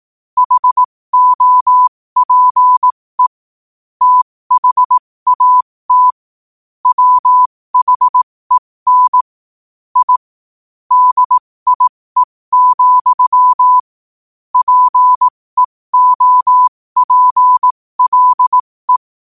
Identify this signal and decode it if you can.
Morse code, heard as a keyed tone, about 18 wpm, spaced as for 15 wpm, 1000 Hz. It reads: HOPE THAT WHEN I DIE, PEOPLE
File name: qotdfiles / 15wpm